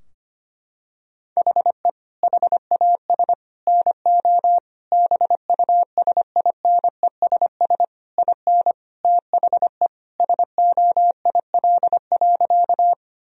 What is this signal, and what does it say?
Morse code, heard as a keyed tone, about 25 wpm, 700 Hz. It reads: HE HAS NO BUSINESS IN THE SOIL.
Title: Sentences from Top 1000 Words - Review Part 4 - 25wpm